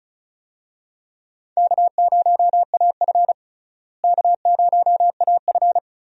Morse code, heard as a keyed tone, about 34 wpm, 700 Hz. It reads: K0AF K0AF
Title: USA Call Signs - 4 Characters - Repeated 2x - 35wpm